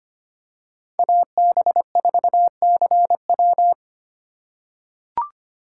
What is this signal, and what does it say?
Morse code, heard as a keyed tone, about 25 wpm, 700 Hz. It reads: AB4CW
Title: USA Call Signs - 5 Characters - 3x Character Spacing - 25wpm